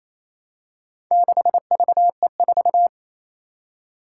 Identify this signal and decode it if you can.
Morse code, heard as a keyed tone, about 29 wpm, 700 Hz. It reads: 6VE4
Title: Four Challenging Letter-Number Combinations 28wpm